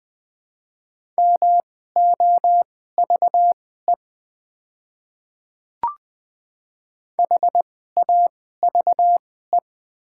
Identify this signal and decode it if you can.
Morse code, heard as a keyed tone, about 20 wpm, spaced as for 14 wpm, 700 Hz. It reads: MOVE HAVE
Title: 2x Character Spacing - Top 300 Words - 20wpm